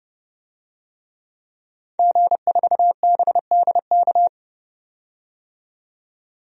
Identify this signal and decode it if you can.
Morse code, heard as a keyed tone, about 30 wpm, 700 Hz. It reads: G4BDK
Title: DXCC-100 Call Signs - 5 Characters - 30wpm